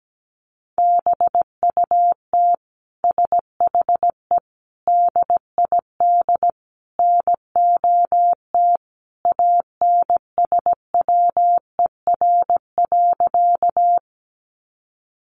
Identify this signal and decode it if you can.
Morse code, heard as a keyed tone, about 17 wpm, 700 Hz. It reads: BUT SHE DID NOT ANSWER.
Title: Sentences from Top 600 Words 17wpm